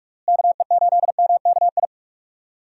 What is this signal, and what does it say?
Morse code, heard as a keyed tone, about 46 wpm, 700 Hz. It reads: KE8MKI
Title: USA Call Signs - 6 Characters - Repeated 3x - 45wpm